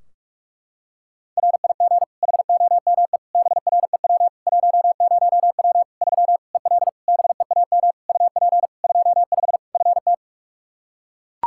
Morse code, heard as a keyed tone, about 41 wpm, 700 Hz. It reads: RIG HOME BREW 10W 3 EL BEAM UP 25 FT
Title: K1USN SST QSO Practice 45wpm